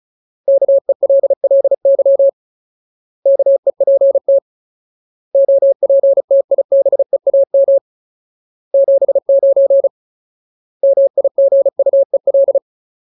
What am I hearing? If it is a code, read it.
Morse code, heard as a keyed tone, about 34 wpm, 550 Hz. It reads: KELLY KEPT OPTIBEAM 79 MIGUEL
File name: Ultimate Word List / 35WPM Ultimate